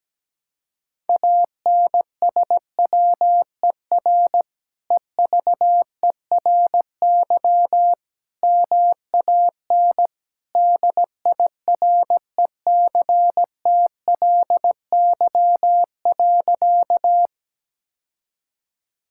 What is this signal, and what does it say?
Morse code, heard as a keyed tone, about 17 wpm, 700 Hz. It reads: ANSWER EVERY MAN DIRECTLY.
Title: Sentences from Top 1000 Words - Review Part 5 - 17wpm